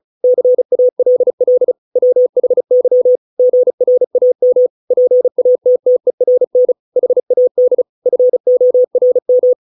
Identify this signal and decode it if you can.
Morse code, heard as a keyed tone, about 34 wpm, 500 Hz. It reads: CALL WHY GRAM PATTERN HAD FORM